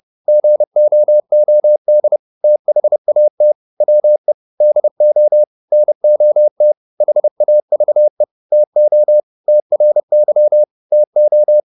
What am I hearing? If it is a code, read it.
Morse code, heard as a keyed tone, about 30 wpm, 600 Hz. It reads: GOOD THAT WE DO NOT HAVE TO TRY TO